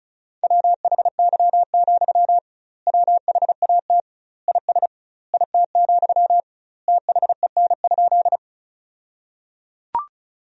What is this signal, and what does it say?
Morse code, heard as a keyed tone, about 36 wpm, 700 Hz. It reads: WHY, WHAT IS IT, THEN?
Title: 2x Word-Spacing - Sentences from Top 200 Words - 35wpm